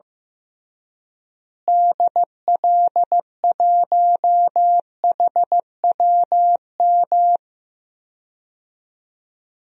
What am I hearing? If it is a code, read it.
Morse code, heard as a keyed tone, about 14 wpm, 700 Hz. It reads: DL1HWM
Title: DXCC-100 Call Signs - 6 Characters - Repeated 3x - 15wpm